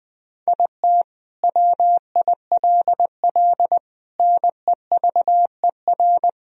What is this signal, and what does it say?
Morse code, heard as a keyed tone, about 20 wpm, 700 Hz. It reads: IT WILL NEVER
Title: Sentences from Top 1000 Words - Review Part 2 - 20wpm